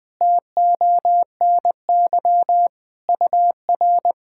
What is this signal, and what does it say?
Morse code, heard as a keyed tone, about 20 wpm, 700 Hz. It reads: TONY UR
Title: SOTA QSO Practice 20wpm